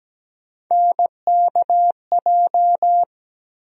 Morse code, heard as a keyed tone, about 17 wpm, 700 Hz. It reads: NKJ
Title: USA Call Sign Postfixes - 3 Letters 17wpm